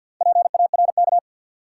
Morse code, heard as a keyed tone, about 52 wpm, 700 Hz. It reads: PARK
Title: Sentences from Top 1300 Words - New Word Drill - 50wpm